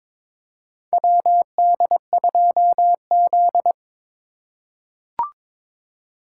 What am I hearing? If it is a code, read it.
Morse code, heard as a keyed tone, about 22 wpm, 700 Hz. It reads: WD2Z
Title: USA Call Signs - 2x Character Spacing - 22wpm